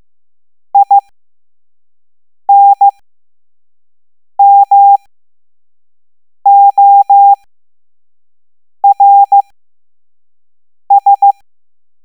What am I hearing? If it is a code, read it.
Morse code, heard as a keyed tone, about 15 wpm, 800 Hz. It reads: I N M O R S